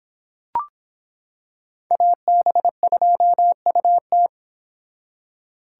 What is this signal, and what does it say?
Morse code, heard as a keyed tone, about 26 wpm, 700 Hz. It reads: AB2UT